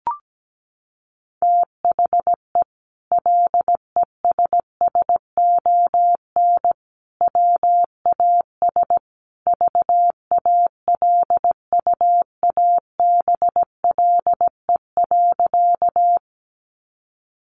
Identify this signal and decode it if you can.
Morse code, heard as a keyed tone, about 17 wpm, 700 Hz. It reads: THE LESSON WAS VALUABLE.
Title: Sentences from Top 1200 Words 17wpm